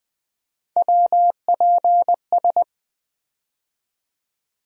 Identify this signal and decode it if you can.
Morse code, heard as a keyed tone, about 20 wpm, 700 Hz. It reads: WPS
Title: USA Call Sign Postfixes - 3 Letters 20wpm